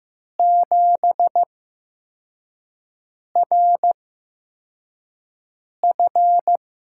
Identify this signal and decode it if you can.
Morse code, heard as a keyed tone, about 15 wpm, spaced as for 4 wpm, 700 Hz. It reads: 7RF
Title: USA Call Signs - 8x Character Spacing - 15wpm